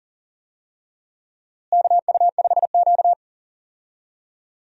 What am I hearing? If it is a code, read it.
Morse code, heard as a keyed tone, about 40 wpm, 700 Hz. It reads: KU5Q
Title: USA Call Signs - 4 Characters - Repeated 3x - 40wpm